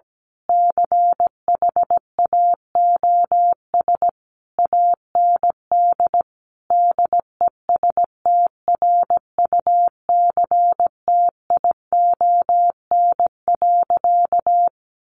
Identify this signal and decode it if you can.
Morse code, heard as a keyed tone, about 17 wpm, 700 Hz. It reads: CHAOS AND DESTRUCTION.
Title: A Sound Of Thunder 17wpm